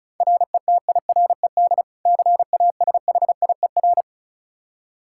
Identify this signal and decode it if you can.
Morse code, heard as a keyed tone, about 35 wpm, 700 Hz. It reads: RETIRED CASHIER
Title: K1USN SST QSO Practice 35wpm